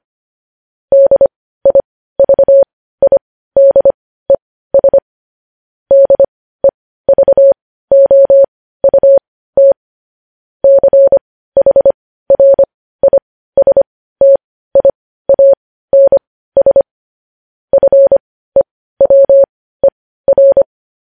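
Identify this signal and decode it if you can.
Morse code, heard as a keyed tone, about 25 wpm, spaced as for 15 wpm, 550 Hz. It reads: DIVIDES DEVOUT CHRISTIANS FEWER
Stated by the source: Morse practice sending news headlines at 15 wpm.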